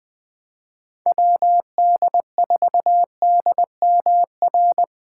Morse code, heard as a keyed tone, about 24 wpm, 700 Hz. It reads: WD4DMR
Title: Speed Racing - USA 2x3 Call Signs 20wpm